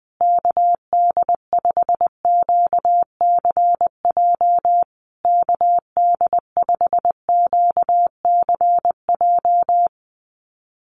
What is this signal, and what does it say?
Morse code, heard as a keyed tone, about 20 wpm, 700 Hz. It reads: KD5QCJ KD5QCJ
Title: USA Call Signs Repeated 3x 20wpm